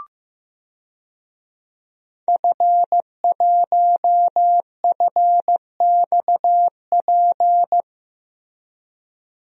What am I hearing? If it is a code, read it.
Morse code, heard as a keyed tone, about 16 wpm, 700 Hz. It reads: F1FXP
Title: DXCC-100 Call Signs - 5 Characters - Repeated 3x - 15wpm